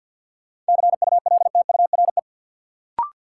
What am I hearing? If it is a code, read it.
Morse code, heard as a keyed tone, about 53 wpm, 700 Hz. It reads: CULTURE
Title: Sentences from Top 1000 Words - Review Part 6 - New Word Drill - 50wpm